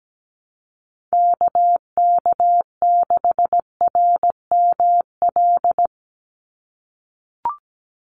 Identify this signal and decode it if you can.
Morse code, heard as a keyed tone, about 17 wpm, 700 Hz. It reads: KK6RML
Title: USA Call Signs 17wpm